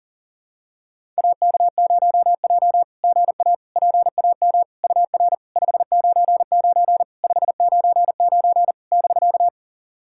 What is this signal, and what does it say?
Morse code, heard as a keyed tone, about 40 wpm, 690 Hz. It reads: AK0J GA PAM UR 599 599 <BK>
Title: SOTA QSO Practice 40wpm